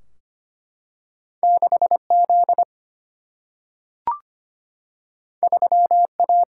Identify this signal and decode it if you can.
Morse code, heard as a keyed tone, about 25 wpm, 700 Hz. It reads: 6Z 3A
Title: Single-Letter Single-Number 25wpm